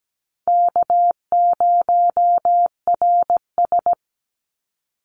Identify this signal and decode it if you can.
Morse code, heard as a keyed tone, about 17 wpm, 700 Hz. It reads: K0RS
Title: NA SOTA Chaser Callsigns 17wpm